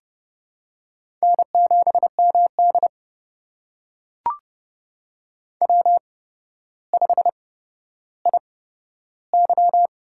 Morse code, heard as a keyed tone, about 31 wpm, spaced as for 13 wpm, 700 Hz. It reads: N7MD W5IY
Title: USA Call Signs - 4 Characters - 8x Character Spacing - 30wpm